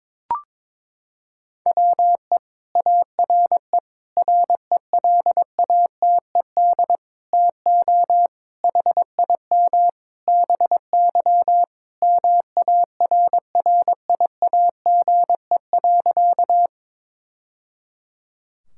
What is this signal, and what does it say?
Morse code, heard as a keyed tone, about 22 wpm, 700 Hz. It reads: WE ARE RELATED TO HIM BY MARRIAGE.
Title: Sentences from Top 1000 Words - Review Part 5 - 22wpm